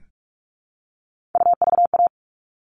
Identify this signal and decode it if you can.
Morse code, heard as a keyed tone, about 44 wpm, 700 Hz. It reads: UVA
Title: All Cut Numbers - 3-Digit 45wpm